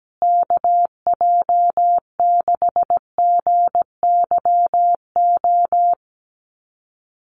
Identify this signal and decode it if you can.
Morse code, heard as a keyed tone, about 17 wpm, 700 Hz. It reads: KJ6GYO
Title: USA Call Signs Repeated 2x 17wpm